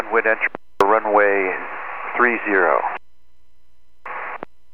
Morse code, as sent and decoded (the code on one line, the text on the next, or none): none